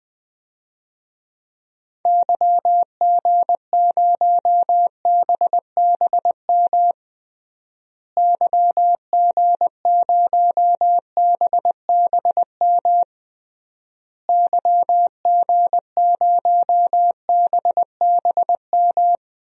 Morse code, heard as a keyed tone, about 20 wpm, 700 Hz. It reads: YG0BBM YG0BBM YG0BBM
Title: DXCC-100 Call Signs - 6 Characters - Repeated 3x - 20wpm